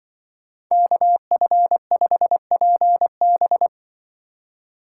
Morse code, heard as a keyed tone, about 24 wpm, 700 Hz. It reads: KF5PB
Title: Speed Racing - USA 2x2 Call Signs 20wpm